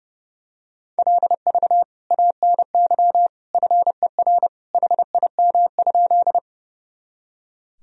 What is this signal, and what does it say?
Morse code, heard as a keyed tone, about 32 wpm, 700 Hz. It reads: LV ANY FER HIM?
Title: Sentences from Top 400 Words and Abbreviations 30wpm